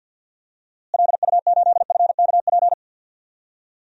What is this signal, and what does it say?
Morse code, heard as a keyed tone, about 51 wpm, 700 Hz. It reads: LU8FKP
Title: DXCC-100 Call Signs - 6 Characters - Repeated 3x - 50wpm